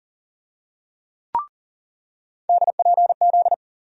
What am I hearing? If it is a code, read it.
Morse code, heard as a keyed tone, about 40 wpm, 700 Hz. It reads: DPZ
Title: USA Call Sign Postfixes - 3 Letters 40wpm